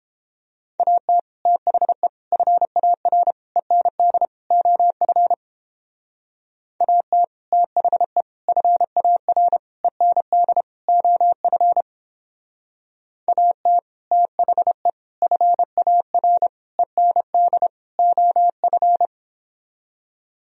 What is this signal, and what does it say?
Morse code, heard as a keyed tone, about 29 wpm, 700 Hz. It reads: AT THE FAR END OF AT THE FAR END OF AT THE FAR END OF